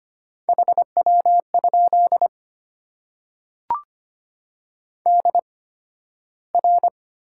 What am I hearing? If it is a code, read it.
Morse code, heard as a keyed tone, about 25 wpm, spaced as for 13 wpm, 700 Hz. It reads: HW? DR
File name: Top 90 QSO Elements - 8x Character Spacing - 25wpm